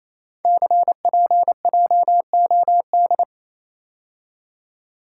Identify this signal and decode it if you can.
Morse code, heard as a keyed tone, about 29 wpm, 700 Hz. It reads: CPJOD
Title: Code Groups - 5 Letters - 28wpm